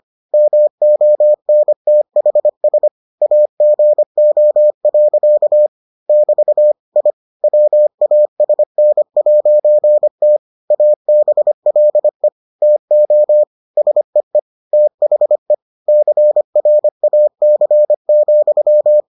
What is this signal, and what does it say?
Morse code, heard as a keyed tone, about 25 wpm, 600 Hz. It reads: MONTHS AGO. <BT> I WASN'T ABLE TO SEE THE CRAC,